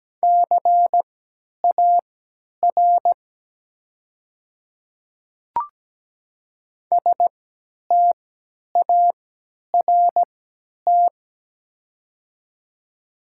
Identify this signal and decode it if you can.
Morse code, heard as a keyed tone, about 17 wpm, spaced as for 10 wpm, 700 Hz. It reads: CAR START
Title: 3x Character Spacing - Top 500 Words - 17wpm